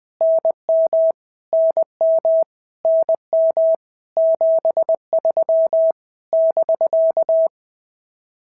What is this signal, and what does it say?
Morse code, heard as a keyed tone, about 20 wpm, 650 Hz. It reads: NM NM NM 73 <BK>
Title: POTA QSO Practice - 20wpm